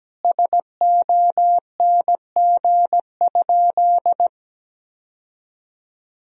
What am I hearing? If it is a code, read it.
Morse code, heard as a keyed tone, about 17 wpm, 700 Hz. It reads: SONG?
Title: Sentences from Top 1000 Words - Review Part 2 - 17wpm